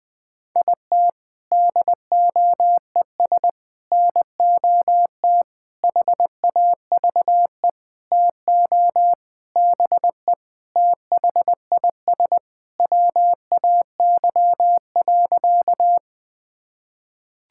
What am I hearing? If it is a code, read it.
Morse code, heard as a keyed tone, about 19 wpm, 700 Hz. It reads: IT DOES NOT HAVE TO BE THIS WAY.
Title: Sentences with 5-Word Phrases - Spoken Beginning - 20wpm